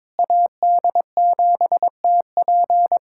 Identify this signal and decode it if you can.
Morse code, heard as a keyed tone, about 22 wpm, 700 Hz. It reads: AD7TP
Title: USA Callsign DE Callsign 22wpm